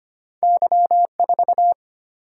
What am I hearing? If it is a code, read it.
Morse code, heard as a keyed tone, about 25 wpm, 700 Hz. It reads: Y 4
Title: Single-Letter Single-Number 25wpm